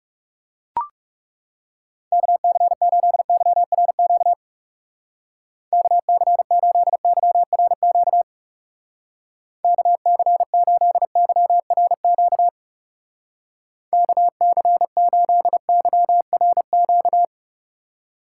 Morse code, heard as a keyed tone, about 36 wpm, 700 Hz. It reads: KC8YRQ KC8YRQ KC8YRQ KC8YRQ
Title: Speed Racing - USA 2x3 Call Signs 30wpm